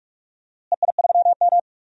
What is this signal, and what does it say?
Morse code, heard as a keyed tone, about 46 wpm, 690 Hz. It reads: EI3M
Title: DXCC-100 Call Signs - 4 Characters - Repeated 3x - 45wpm